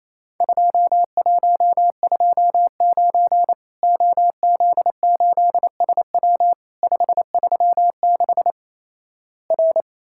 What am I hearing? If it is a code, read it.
Morse code, heard as a keyed tone, about 28 wpm, 700 Hz. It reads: 2129 OZ8SW 536 R
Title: Worked All Europe DX Contest - QTC Exchange - 28wpm